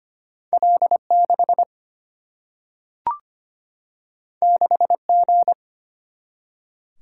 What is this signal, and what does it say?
Morse code, heard as a keyed tone, about 25 wpm, 700 Hz. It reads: L6 6G
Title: Single-Letter Single-Number 25wpm